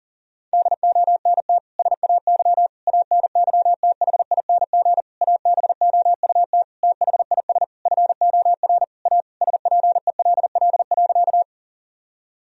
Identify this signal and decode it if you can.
Morse code, heard as a keyed tone, about 40 wpm, 700 Hz. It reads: DONT SAY ANYTHING ABOUT THIS FOR A SPELL.
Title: Sentences from Top 1000 Words - Review Part 1 - 40wpm